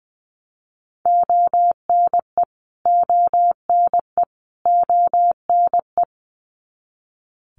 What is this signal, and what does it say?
Morse code, heard as a keyed tone, about 20 wpm, 700 Hz. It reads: ONE ONE ONE
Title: Top 200 Words Repeated 20wpm